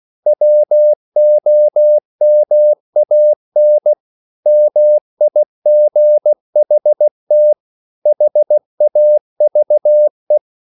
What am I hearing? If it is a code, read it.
Morse code, heard as a keyed tone, about 16 wpm, 600 Hz. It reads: WOMAN MIGHT HAVE